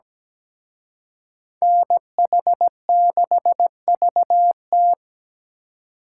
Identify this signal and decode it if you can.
Morse code, heard as a keyed tone, about 20 wpm, 700 Hz. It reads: NH6VT
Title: Speed Racing - USA 2x2 Call Signs 17wpm